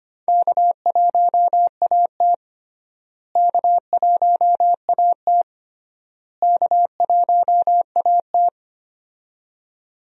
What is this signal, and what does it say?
Morse code, heard as a keyed tone, about 25 wpm, 700 Hz. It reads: K1AT K1AT K1AT
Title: USA Call Signs - 4 Characters - Repeated 3x - 25wpm